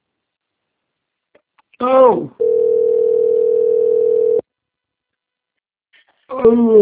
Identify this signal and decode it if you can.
Ringback tone (North America), 440 + 480 Hz, with speech.
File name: rg-1600-unknown- - - .76818